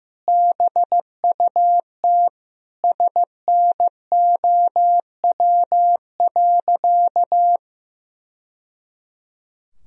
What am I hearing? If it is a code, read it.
Morse code, heard as a keyed tone, about 16 wpm, 700 Hz. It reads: BUT SNOW.
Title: Sentences from Top 1000 Words - Review Part 3 - 15wpm